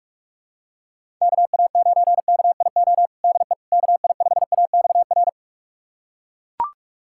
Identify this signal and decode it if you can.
Morse code, heard as a keyed tone, about 41 wpm, 700 Hz. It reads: KA9KIO DE KI5AXR
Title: USA Callsign DE Callsign - 6 Characters - 45wpm